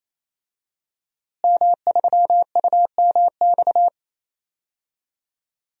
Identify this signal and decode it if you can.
Morse code, heard as a keyed tone, about 28 wpm, 700 Hz. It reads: M3UMX
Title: DXCC-100 Call Signs - 5 Characters - Repeated 3x - 28wpm